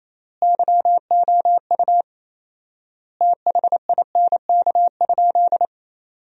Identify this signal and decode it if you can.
Morse code, heard as a keyed tone, about 28 wpm, 700 Hz. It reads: YOU THINK?
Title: 4x Word-Spacing - Sentences from Top 200 Words - 28wpm